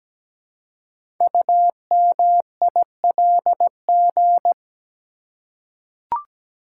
Morse code, heard as a keyed tone, about 17 wpm, 700 Hz. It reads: UMILG
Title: Code Groups - 5 Letters - 17wpm